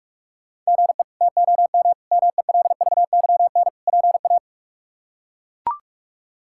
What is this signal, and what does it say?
Morse code, heard as a keyed tone, about 41 wpm, 700 Hz. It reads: GE TOM MELVYN PA
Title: K1USN SST QSO Practice - 45wpm